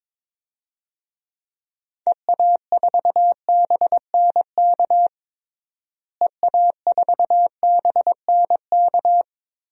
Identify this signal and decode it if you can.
Morse code, heard as a keyed tone, about 22 wpm, 700 Hz. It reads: EA4BNK EA4BNK
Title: DXCC-100 Call Signs - 6 Characters - Repeated 2x - 22wpm